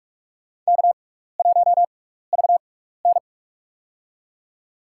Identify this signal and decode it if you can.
Morse code, heard as a keyed tone, about 46 wpm, spaced as for 16 wpm, 700 Hz. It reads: K1VN
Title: USA Call Signs - 4 Characters - 6x Character Spacing - 45wpm